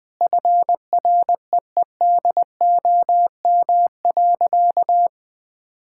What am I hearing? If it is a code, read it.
Morse code, heard as a keyed tone, about 20 wpm, 700 Hz. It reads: FREEDOM.
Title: Sentences from Top 1600 Words 20wpm